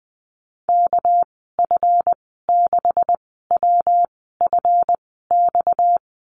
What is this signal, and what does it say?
Morse code, heard as a keyed tone, about 20 wpm, spaced as for 14 wpm, 700 Hz. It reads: KF6WFX
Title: 238 Lesson 38g - Letter q - Call Signs 20wpm